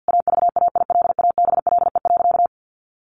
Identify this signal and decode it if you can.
Morse code, heard as a keyed tone, about 50 wpm, 700 Hz. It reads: AVAILABLE.
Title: Sentences from Top 600 Words 50wpm